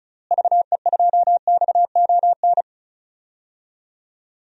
Morse code, heard as a keyed tone, about 34 wpm, 700 Hz. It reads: VE2XON
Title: DXCC-100 Call Signs - 6 Characters - Repeated 2x - 35wpm